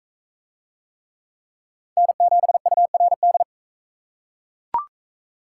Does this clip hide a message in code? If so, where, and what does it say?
Yes: Morse code from 2.0 to 3.4 s, heard as a keyed tone, about 40 wpm, 700 Hz. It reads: N7URD